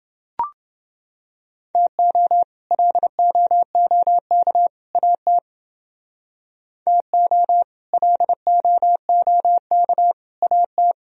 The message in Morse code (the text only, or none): TO LOOK AT TO LOOK AT